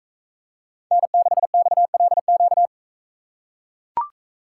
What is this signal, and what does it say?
Morse code, heard as a keyed tone, about 42 wpm, 700 Hz. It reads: N6XLQ